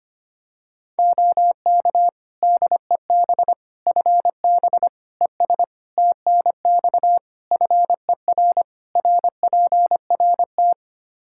Morse code, heard as a keyed tone, about 25 wpm, 700 Hz. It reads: OK DEB FB ES TNX FER RPRT
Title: Practice QSO - Exchange 2 - Info - 25wpm